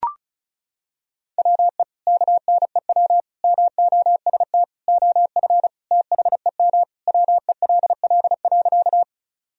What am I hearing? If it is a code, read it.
Morse code, heard as a keyed tone, about 34 wpm, 700 Hz. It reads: WE KNEW MOST OF THEM WELL.
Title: Sentences from Top 1000 Words - Review Part 1 - 35wpm